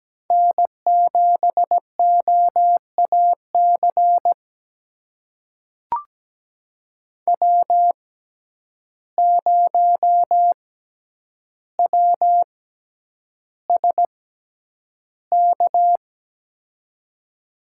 Morse code, heard as a keyed tone, about 17 wpm, spaced as for 9 wpm, 700 Hz. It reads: N7OAC W0WSK
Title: USA Call Signs - 6x Character Spacing - 17wpm